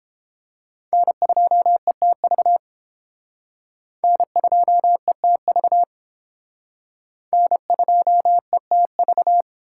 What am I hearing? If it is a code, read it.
Morse code, heard as a keyed tone, about 31 wpm, 700 Hz. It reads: N2ETV N2ETV N2ETV